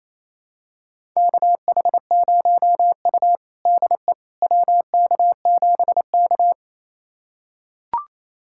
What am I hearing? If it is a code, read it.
Morse code, heard as a keyed tone, about 28 wpm, 700 Hz. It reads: KH0U DE WK7K
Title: USA Callsign DE Callsign - 4 Characters - 28wpm